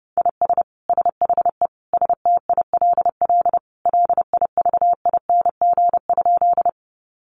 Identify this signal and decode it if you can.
Morse code, heard as a keyed tone, about 30 wpm, 700 Hz. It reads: IS SHE STILL LIVING?
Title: Sentences from Top 1100 Words 30wpm